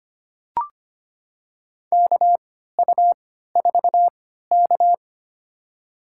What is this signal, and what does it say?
Morse code, heard as a keyed tone, about 25 wpm, spaced as for 14 wpm, 700 Hz. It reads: KU4K
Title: USA Call Signs - 4 Characters - 3x Character Spacing - 25wpm